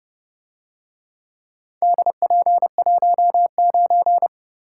Morse code, heard as a keyed tone, about 29 wpm, 700 Hz. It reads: DP19